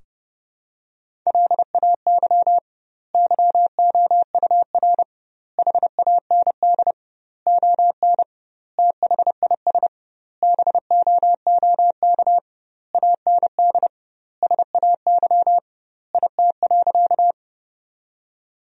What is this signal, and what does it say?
Morse code, heard as a keyed tone, about 30 wpm, 700 Hz. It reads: LAY YOUR HAND ON THIS BOOK AND SAY IT.
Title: 2x Word-Spacing - Sentences from Top 400 Words - 30wpm